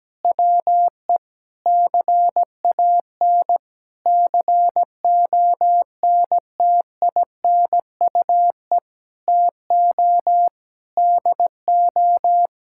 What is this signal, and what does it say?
Morse code, heard as a keyed tone, about 17 wpm, 700 Hz. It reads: WE CAN CONTINUE TO DO
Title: Sentences from Top 1000 Words - Review Part 6 - 17wpm